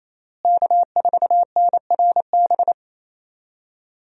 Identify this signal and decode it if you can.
Morse code, heard as a keyed tone, about 28 wpm, 700 Hz. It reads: K4NRB
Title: USA Call Signs - 5 Characters - Repeated 3x - 28wpm